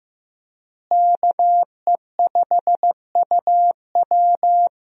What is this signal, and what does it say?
Morse code, heard as a keyed tone, about 16 wpm, 700 Hz. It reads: KE5UW
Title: USA Call Signs - 5 Characters - Repeated 2x - 15wpm